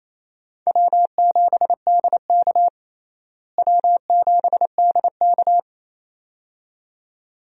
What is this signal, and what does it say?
Morse code, heard as a keyed tone, about 28 wpm, 700 Hz. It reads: W7DK W7DK
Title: USA Call Signs - 4 Characters - Repeated 2x - 28wpm